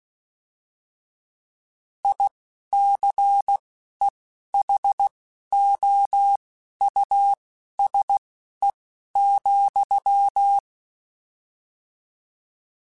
Morse code, heard as a keyed tone, about 16 wpm, spaced as for 11 wpm, 800 Hz. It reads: ICEHOUSE,
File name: General / General-mp3s / Houses @10